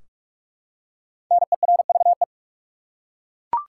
Morse code, heard as a keyed tone, about 42 wpm, 700 Hz. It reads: NERVE
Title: Five Letter Words - 45wpm